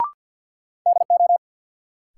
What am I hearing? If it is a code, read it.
Morse code, heard as a keyed tone, about 53 wpm, 700 Hz. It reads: DO
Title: Sentences from Top 100 Words - New Word Drill - 50wpm